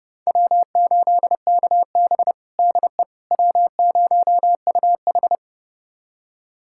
Morse code, heard as a keyed tone, about 30 wpm, 700 Hz. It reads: W8KB DE W0UH
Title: USA Callsign DE Callsign - 4 Characters - 30wpm